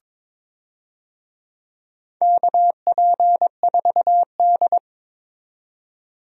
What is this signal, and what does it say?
Morse code, heard as a keyed tone, about 22 wpm, 700 Hz. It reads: KP4D
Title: DXCC-100 Call Signs - 4 Characters - 22wpm